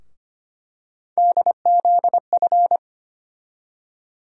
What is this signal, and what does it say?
Morse code, heard as a keyed tone, about 25 wpm, 700 Hz. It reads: DZF
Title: USA Call Sign Postfixes - 3 Letters 25wpm